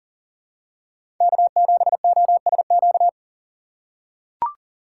Morse code, heard as a keyed tone, about 40 wpm, 700 Hz. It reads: K7OSQ
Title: USA Call Signs - 5 Characters - 3x Character Spacing - 40wpm